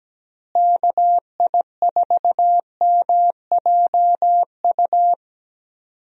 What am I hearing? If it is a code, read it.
Morse code, heard as a keyed tone, about 17 wpm, 700 Hz. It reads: KI4MJU
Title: USA Call Signs - 6 Characters - Repeated 3x - 17wpm